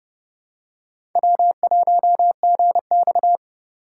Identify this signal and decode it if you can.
Morse code, heard as a keyed tone, about 31 wpm, 700 Hz. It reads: W1GX
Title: USA Call Signs - 4 Characters - 3x Character Spacing - 30wpm